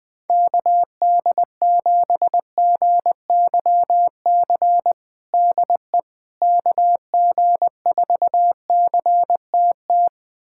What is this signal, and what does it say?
Morse code, heard as a keyed tone, about 20 wpm, 700 Hz. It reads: KD7GYC DE KG4CTT
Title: USA Callsign DE Callsign - 6 Characters - 20wpm